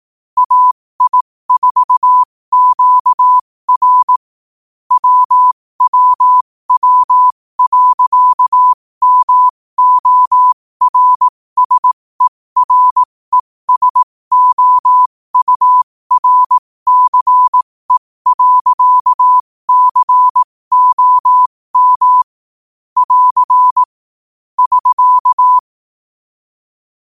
Morse code, heard as a keyed tone, about 18 wpm, spaced as for 15 wpm, 1000 Hz. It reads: AI4QR WWW.MORSERESOURCE.COM <AR> <SK>
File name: qotdfiles / 15wpm